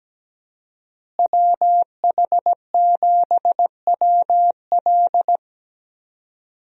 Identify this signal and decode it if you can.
Morse code, heard as a keyed tone, about 17 wpm, 700 Hz. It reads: WH7WL
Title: USA Call Signs - 5 Characters - Repeated 3x - 17wpm